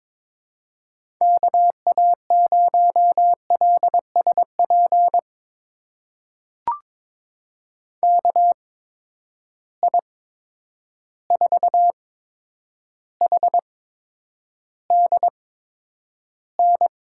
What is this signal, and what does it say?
Morse code, heard as a keyed tone, about 22 wpm, spaced as for 9 wpm, 700 Hz. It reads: KA0LSP KI4HDN
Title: USA Call Signs - 6 Characters - 8x Character Spacing - 22wpm